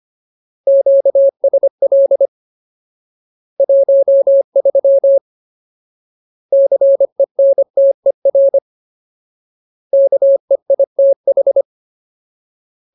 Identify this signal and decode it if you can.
Morse code, heard as a keyed tone, about 25 wpm, 550 Hz. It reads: QSL 13 CENTER KEITH
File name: Ultimate Word List / 25WPM Ultimate